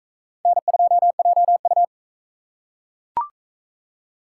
Morse code, heard as a keyed tone, about 43 wpm, 700 Hz. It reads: N2JU